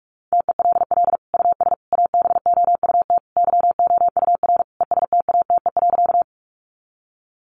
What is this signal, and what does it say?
Morse code, heard as a keyed tone, about 44 wpm, 700 Hz. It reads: TELL US ABOUT YOUR ESTATE.
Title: Sentences from Top 1200 Words 45wpm